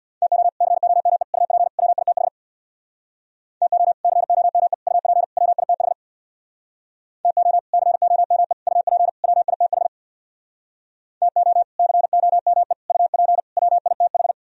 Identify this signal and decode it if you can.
Morse code, heard as a keyed tone, about 63 wpm, 700 Hz. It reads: TO COME UP WITH TO COME UP WITH TO COME UP WITH TO COME UP WITH